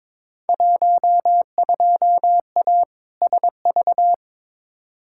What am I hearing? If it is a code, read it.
Morse code, heard as a keyed tone, about 22 wpm, 700 Hz. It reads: 12A SV
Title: Field Day Exchanges Encore - 22wpm